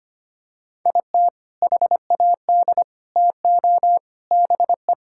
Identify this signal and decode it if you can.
Morse code, heard as a keyed tone, about 25 wpm, 700 Hz. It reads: IT HAD TO BE
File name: practice / 25wpm / Sentences with 3-Word Phrases - 25wpm